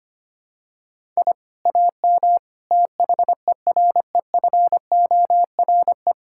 Morse code, heard as a keyed tone, about 25 wpm, 700 Hz. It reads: I AM THEREFORE
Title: Sentences from Top 1000 Words - Review Part 6 - 25wpm